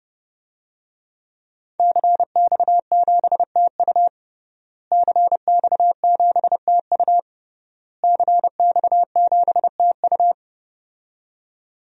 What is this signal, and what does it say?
Morse code, heard as a keyed tone, about 30 wpm, 700 Hz. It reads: CX7TU CX7TU CX7TU
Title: DXCC-100 Call Signs - 5 Characters - Repeated 3x - 30wpm